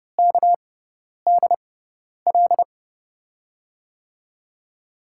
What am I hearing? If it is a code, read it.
Morse code, heard as a keyed tone, about 30 wpm, spaced as for 10 wpm, 700 Hz. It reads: KDL
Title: USA Call Signs - 6x Character Spacing - 30wpm